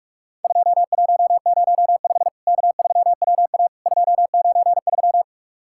Morse code, heard as a keyed tone, about 44 wpm, 700 Hz. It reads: 2105 K3WA 293
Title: Worked All Europe DX Contest - QTC Exchange - 45wpm